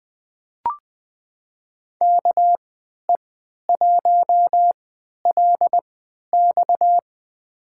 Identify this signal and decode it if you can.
Morse code, heard as a keyed tone, about 20 wpm, spaced as for 11 wpm, 700 Hz. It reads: KE1LX
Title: USA Call Signs - 5 Characters - 3x Character Spacing - 20wpm